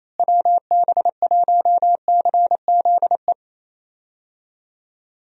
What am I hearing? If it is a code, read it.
Morse code, heard as a keyed tone, about 27 wpm, 700 Hz. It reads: WB1CZE
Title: USA Call Signs - 6 Characters - Repeated 3x - 28wpm